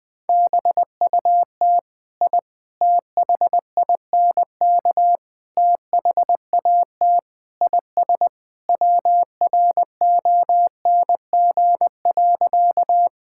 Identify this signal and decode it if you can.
Morse code, heard as a keyed tone, about 20 wpm, 700 Hz. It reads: BUT I THINK THAT IS WRONG.
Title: Sentences with 4-Word Phrases - Spoken Beginning - 20wpm